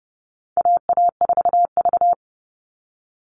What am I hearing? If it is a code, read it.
Morse code, heard as a keyed tone, about 30 wpm, 700 Hz. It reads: AA4V
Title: All Cut Numbers - 4-Digit 30wpm